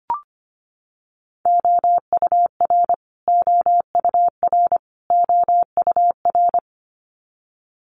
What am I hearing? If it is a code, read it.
Morse code, heard as a keyed tone, about 25 wpm, 700 Hz. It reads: OUR OUR OUR
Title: Top 500 Words Repeated 25wpm